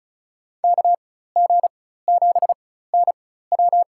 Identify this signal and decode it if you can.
Morse code, heard as a keyed tone, about 36 wpm, spaced as for 16 wpm, 700 Hz. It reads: KG7NW
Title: USA Call Signs - 5 Characters - 4x Character Spacing - 35wpm